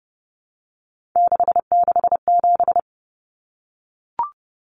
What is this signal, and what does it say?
Morse code, heard as a keyed tone, about 30 wpm, 700 Hz. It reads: 667
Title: Common Cut Numbers - 3-Digit 30wpm